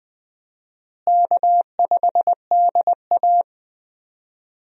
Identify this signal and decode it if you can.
Morse code, heard as a keyed tone, about 21 wpm, 700 Hz. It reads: K5DA
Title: USA Call Signs - 4 Characters - Repeated 2x - 20wpm